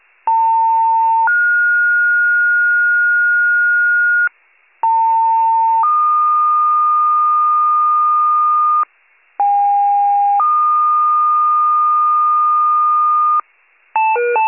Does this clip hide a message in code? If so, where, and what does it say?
Yes: two-tone page from 0.3 to 13.4 s, heard as a one-second tone, then a three-second tone, about 900 then 1430 Hz, about 900 then 1220 Hz, about 810 then 1220 Hz.